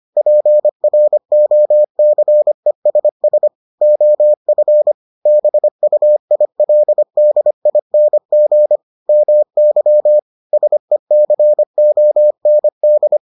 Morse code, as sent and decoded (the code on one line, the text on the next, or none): .--. .-. --- -.-. ... ... / --- ..-. / -... ..- .. .-.. -.. .. -. --. / -- -.-- / ... -.-. --- -. -..
PROCESS OF BUILDING MY SECOND